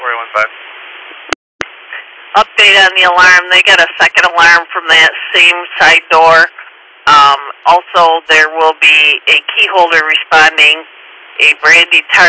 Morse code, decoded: none

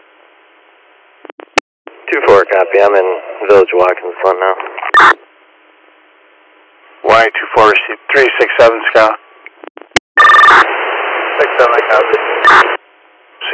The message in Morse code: none

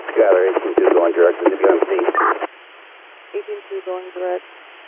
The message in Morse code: none

